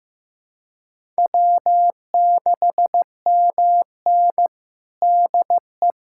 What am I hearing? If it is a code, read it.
Morse code, heard as a keyed tone, about 15 wpm, 700 Hz. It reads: W6MN DE
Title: USA Callsign DE Callsign - 4 Characters - 15wpm